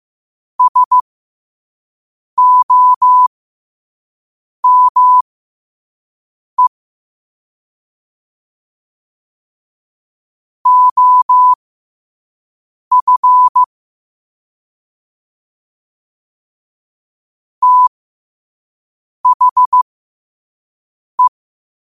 Morse code, heard as a keyed tone, about 15 wpm, spaced as for 5 wpm, 1000 Hz. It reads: SOME OF THE
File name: qotdfiles / 5wpm